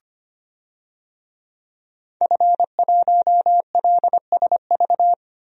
Morse code, heard as a keyed tone, about 25 wpm, 700 Hz. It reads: F1LSV
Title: DXCC-100 Call Signs - 5 Characters - Repeated 2x - 25wpm